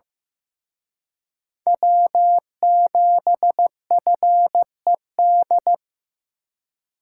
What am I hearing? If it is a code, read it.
Morse code, heard as a keyed tone, about 18 wpm, 700 Hz. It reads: W7FED
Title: Speed Racing - USA 1x3 Call Signs 15wpm